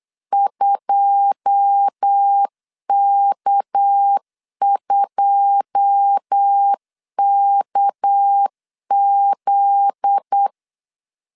Morse code, heard as a keyed tone, about 8 wpm, 800 Hz. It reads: 2K2KZ
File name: CW / DET / 1 9 1500C S C 8wpm 5653